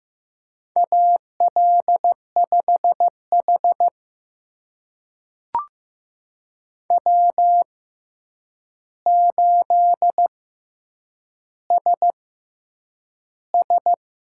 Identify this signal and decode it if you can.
Morse code, heard as a keyed tone, about 15 wpm, spaced as for 8 wpm, 700 Hz. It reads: AL5H W8SS
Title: USA Call Signs - 4 Characters - 6x Character Spacing - 15wpm